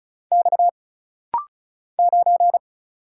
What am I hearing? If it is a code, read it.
Morse code, heard as a keyed tone, about 35 wpm, 700 Hz. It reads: X 9
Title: Single Letter-Number Alt - Rapid-Fire 35wpm